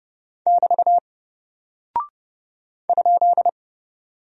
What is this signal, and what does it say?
Morse code, heard as a keyed tone, about 30 wpm, 700 Hz. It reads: <BT> ?
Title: Single Character (Letter:Number:Punctuation:Prosign) 30wpm